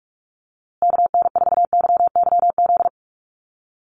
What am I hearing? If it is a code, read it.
Morse code, heard as a keyed tone, about 44 wpm, 690 Hz. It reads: KN4YYZ
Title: USA Call Signs Repeated 3x 45wpm